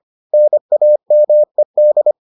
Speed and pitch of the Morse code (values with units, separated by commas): 25 wpm, 600 Hz